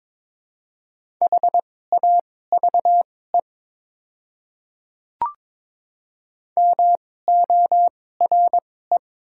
Morse code, heard as a keyed tone, about 22 wpm, spaced as for 16 wpm, 700 Hz. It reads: HAVE MORE
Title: 2x Character Spacing - Top 300 Words - 22wpm